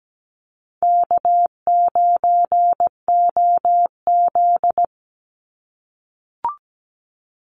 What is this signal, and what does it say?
Morse code, heard as a keyed tone, about 17 wpm, 700 Hz. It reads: K9OZ
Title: NA SOTA Chaser Callsigns 17wpm